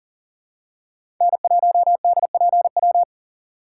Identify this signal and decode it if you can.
Morse code, heard as a keyed tone, about 40 wpm, 700 Hz. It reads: N1DPW
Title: USA Call Signs - 5 Characters - 4x Character Spacing - 40wpm